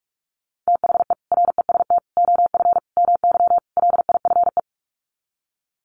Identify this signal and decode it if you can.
Morse code, heard as a keyed tone, about 44 wpm, 700 Hz. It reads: THE REST OF MY LIFE
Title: Sets of 5 Words 45wpm